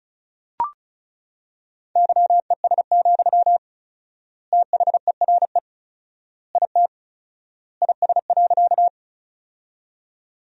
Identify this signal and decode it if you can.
Morse code, heard as a keyed tone, about 36 wpm, 700 Hz. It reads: YES, THERE IT IS.
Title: 4x Word-Spacing - Sentences from Top 500 Words - 35wpm